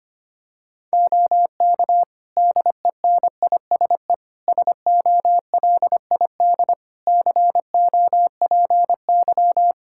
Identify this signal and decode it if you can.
Morse code, heard as a keyed tone, about 25 wpm, 700 Hz. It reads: OK DENISE SOLID COPY
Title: K1USN SST QSO Practice 25wpm